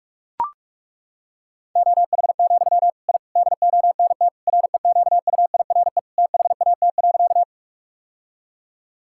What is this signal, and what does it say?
Morse code, heard as a keyed tone, about 41 wpm, 700 Hz. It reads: OH, I DONT REQUIRE THAT.
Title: Sentences from Top 1000 Words - Review Part 6 - 45wpm